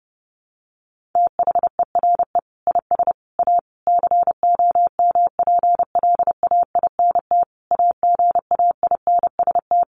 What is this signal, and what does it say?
Morse code, heard as a keyed tone, about 30 wpm, 700 Hz. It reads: THERE IS A COMPLAINT AGAINST
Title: Sentences from Top 1400 Words 30wpm